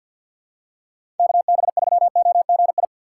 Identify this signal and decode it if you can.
Morse code, heard as a keyed tone, about 50 wpm, 700 Hz. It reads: KB3OGI
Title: USA Call Signs - 6 Characters - 6x Character Spacing - 50wpm